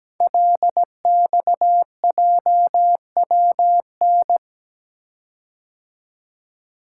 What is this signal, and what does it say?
Morse code, heard as a keyed tone, about 17 wpm, 700 Hz. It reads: LXJWN
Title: Code Groups - 5 Letter:Numbers - 17wpm